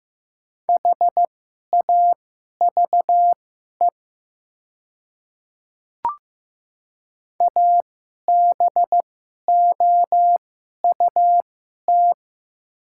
Morse code, heard as a keyed tone, about 15 wpm, spaced as for 11 wpm, 700 Hz. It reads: HAVE ABOUT
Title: 2x Character Spacing - Top 100 Words - 15wpm